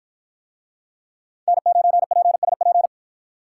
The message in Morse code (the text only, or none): N9PSP